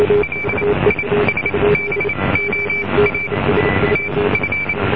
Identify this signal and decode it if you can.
Morse code, heard as a keyed tone, about 28 wpm, spaced as for 21 wpm, 2300 Hz, with noise. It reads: LISBOA TF